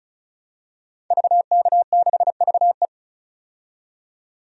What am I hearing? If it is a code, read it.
Morse code, heard as a keyed tone, about 36 wpm, 700 Hz. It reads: VK6VE
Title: DXCC-100 Call Signs - 5 Characters - 35wpm